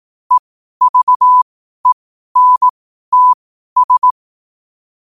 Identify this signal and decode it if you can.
Morse code, heard as a keyed tone, about 18 wpm, spaced as for 12 wpm, 1000 Hz. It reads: EVENTS
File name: qotdfiles / 12wpm